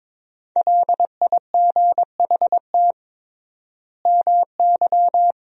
Morse code, heard as a keyed tone, about 22 wpm, 700 Hz. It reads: LIGHT MY
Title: 3x Word-Spacing - Sentences from Top 400 Words - 22wpm